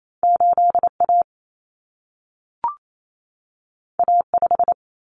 Morse code, heard as a keyed tone, about 28 wpm, 700 Hz. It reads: 8A A5
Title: Common Cut Numbers - 2-Digit 28wpm